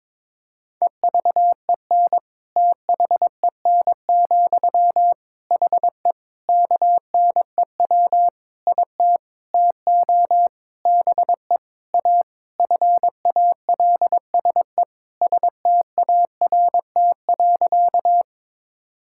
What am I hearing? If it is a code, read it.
Morse code, heard as a keyed tone, about 22 wpm, 700 Hz. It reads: EVEN THEN, HE KNEW IT TO BE A FALSE START.
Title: Sentences from Top 1600 Words 22wpm